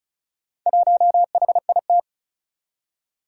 Morse code, heard as a keyed tone, about 36 wpm, 700 Hz. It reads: 1HIT
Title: Four Challenging Letter-Number Combinations 35wpm